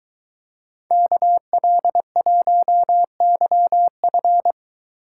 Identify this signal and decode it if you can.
Morse code, heard as a keyed tone, about 23 wpm, 700 Hz. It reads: KL1YF